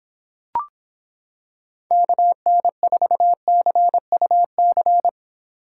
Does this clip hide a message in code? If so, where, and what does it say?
Yes: Morse code from 1.9 to 5.1 s, heard as a keyed tone, about 26 wpm, 700 Hz. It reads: KN4CUC